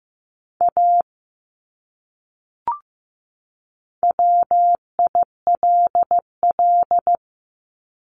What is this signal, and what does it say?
Morse code, heard as a keyed tone, about 15 wpm, 700 Hz. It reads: A WILL
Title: Top 200 Words 15wpm